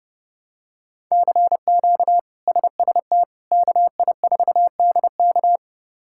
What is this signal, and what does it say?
Morse code, heard as a keyed tone, about 30 wpm, 700 Hz. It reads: CQ SST KI4DK
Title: K1USN SST QSO Practice - 30wpm